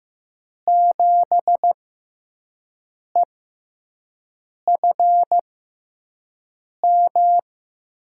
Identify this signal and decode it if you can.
Morse code, heard as a keyed tone, about 15 wpm, spaced as for 5 wpm, 700 Hz. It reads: 7EFM
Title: USA Call Signs - 6 Characters - 6x Character Spacing - 15wpm